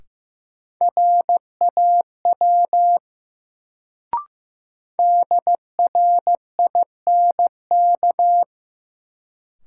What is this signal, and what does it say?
Morse code, heard as a keyed tone, about 16 wpm, 700 Hz. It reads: RAW DRINK
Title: Sentences from Top 1500 Words - New Word Drill - 15wpm